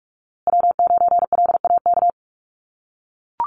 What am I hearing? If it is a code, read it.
Morse code, heard as a keyed tone, about 44 wpm, 700 Hz. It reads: W9LAK
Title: USA Call Signs Repeated 3x 45wpm